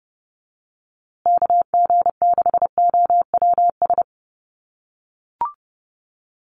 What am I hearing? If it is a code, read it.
Morse code, heard as a keyed tone, about 29 wpm, 700 Hz. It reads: KG6OWS
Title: USA Call Signs Repeated 3x 30wpm